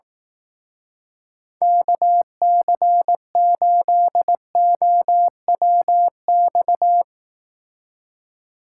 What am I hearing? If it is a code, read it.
Morse code, heard as a keyed tone, about 18 wpm, 700 Hz. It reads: KC8OWX
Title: Speed Racing - USA 2x3 Call Signs 15wpm